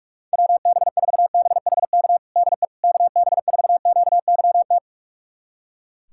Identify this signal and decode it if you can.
Morse code, heard as a keyed tone, about 47 wpm, 700 Hz. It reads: WB4BHK DE KB4QYT
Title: USA Callsign DE Callsign - 6 Characters - 45wpm